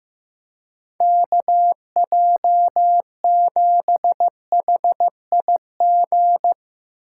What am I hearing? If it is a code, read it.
Morse code, heard as a keyed tone, about 15 wpm, 700 Hz. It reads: KJ7HIG
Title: USA Call Signs - 6 Characters - 4x Character Spacing - 15wpm